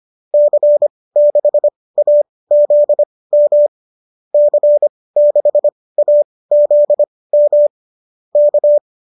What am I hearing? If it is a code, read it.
Morse code, heard as a keyed tone, about 25 wpm, spaced as for 18 wpm, 590 Hz. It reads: C6AZM C6AZM K
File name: qso204 18